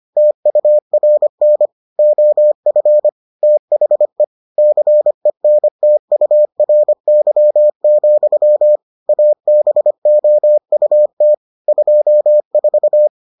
Morse code, heard as a keyed tone, about 25 wpm, 600 Hz. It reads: TURN OF THE CENTURY, ABOUT 24